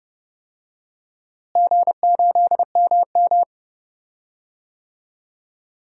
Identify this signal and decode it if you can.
Morse code, heard as a keyed tone, about 30 wpm, 700 Hz. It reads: G8MM
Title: DXCC-100 Call Signs - 4 Characters - Repeated 3x - 30wpm